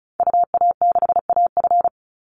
Morse code, heard as a keyed tone, about 34 wpm, 700 Hz. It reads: UA6AF
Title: Call Signs Repeated 2x - 35wpm